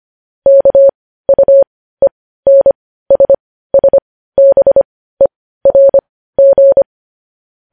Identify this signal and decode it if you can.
Morse code, heard as a keyed tone, about 25 wpm, spaced as for 15 wpm, 550 Hz. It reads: KUENSSBERG
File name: podcast / News headlines-1x-25x15